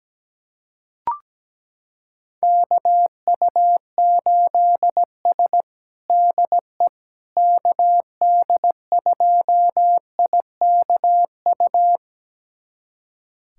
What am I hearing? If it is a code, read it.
Morse code, heard as a keyed tone, about 17 wpm, 700 Hz. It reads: KU8S DE KD2IKU
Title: USA Callsign DE Callsign 17wpm